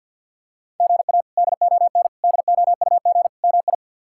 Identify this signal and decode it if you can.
Morse code, heard as a keyed tone, about 52 wpm, 690 Hz. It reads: GA DON DOUG MI
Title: K1USN SST QSO Practice - 50wpm